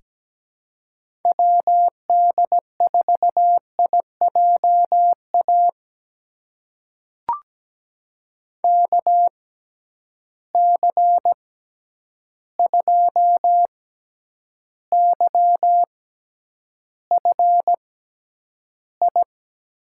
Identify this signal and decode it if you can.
Morse code, heard as a keyed tone, about 17 wpm, spaced as for 9 wpm, 700 Hz. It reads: WD4IJA KC2YFI
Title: USA Call Signs - 6 Characters - 6x Character Spacing - 17wpm